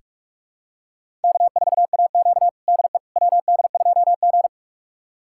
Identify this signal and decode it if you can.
Morse code, heard as a keyed tone, about 46 wpm, 700 Hz. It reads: K4AQ DE WD2G
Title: USA Callsign DE Callsign - 4 Characters - 45wpm